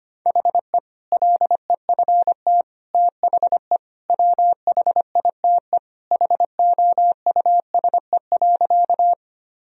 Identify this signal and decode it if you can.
Morse code, heard as a keyed tone, about 25 wpm, 700 Hz. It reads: HE LEFT THE WHITE HOUSE.
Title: Sentences from Top 1000 Words - Review Part 1 - 25wpm